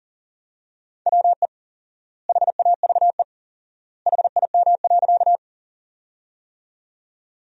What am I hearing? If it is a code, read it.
Morse code, heard as a keyed tone, about 40 wpm, 700 Hz. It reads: WE HAVE HIM.
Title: 4x Word-Spacing - Sentences from Top 100 Words - 40wpm